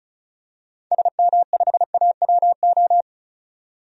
Morse code, heard as a keyed tone, about 34 wpm, 700 Hz. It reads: SM5AWO
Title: DXCC-100 Call Signs - 6 Characters - 35wpm